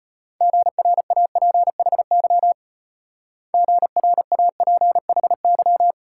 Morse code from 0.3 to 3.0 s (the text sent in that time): GRAPHY